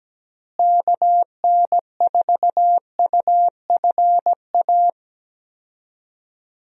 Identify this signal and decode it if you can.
Morse code, heard as a keyed tone, about 17 wpm, 700 Hz. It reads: KN4UFA
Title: USA Call Signs - 6 Characters - Repeated 3x - 17wpm